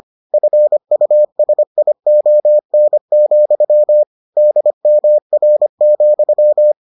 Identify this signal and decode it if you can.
Morse code, heard as a keyed tone, about 25 wpm, 600 Hz. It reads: FUSION, DMR,